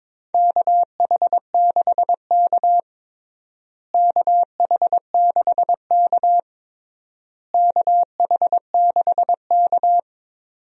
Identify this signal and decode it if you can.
Morse code, heard as a keyed tone, about 22 wpm, 700 Hz. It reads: KH6K KH6K KH6K
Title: DXCC-100 Call Signs - 4 Characters - Repeated 3x - 22wpm